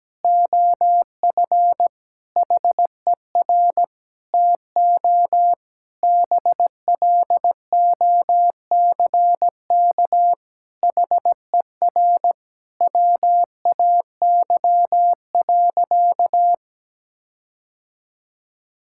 Morse code, heard as a keyed tone, about 17 wpm, 700 Hz. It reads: OF HER TO BLOCK HER WAY.
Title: Sentences from Top 1600 Words 17wpm